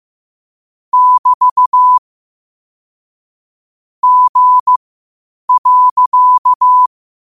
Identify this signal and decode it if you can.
Morse code, heard as a keyed tone, about 15 wpm, spaced as for 8 wpm, 1000 Hz. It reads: <BT> G.
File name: qotdfiles / 8wpm